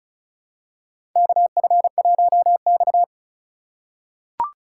Morse code, heard as a keyed tone, about 36 wpm, 700 Hz. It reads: KF1X
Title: USA Call Signs - 4 Characters - 3x Character Spacing - 35wpm